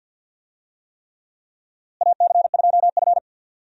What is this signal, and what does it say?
Morse code, heard as a keyed tone, about 52 wpm, 700 Hz. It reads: AC2F
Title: USA Call Signs - 4 Characters - Repeated 3x - 50wpm